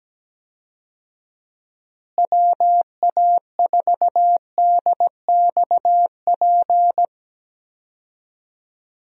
Morse code, heard as a keyed tone, about 17 wpm, 700 Hz. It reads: WA4DXP
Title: USA Call Signs - 6 Characters - Repeated 3x - 17wpm